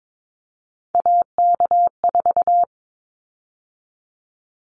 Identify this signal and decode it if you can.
Morse code, heard as a keyed tone, about 22 wpm, 700 Hz. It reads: AK4
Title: Callsign Prefixes for Top 25 DXCC - 22wpm